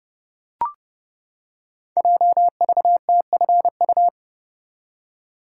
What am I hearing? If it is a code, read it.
Morse code, heard as a keyed tone, about 30 wpm, 700 Hz. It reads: JVTFU
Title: Code Groups - 5 Letter:Numbers - 30wpm